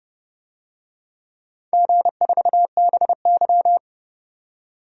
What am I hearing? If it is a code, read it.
Morse code, heard as a keyed tone, about 30 wpm, 700 Hz. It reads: G4BY
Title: DXCC-100 Call Signs - 4 Characters - 30wpm